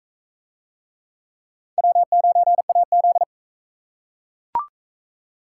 Morse code, heard as a keyed tone, about 41 wpm, 700 Hz. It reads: W9AZ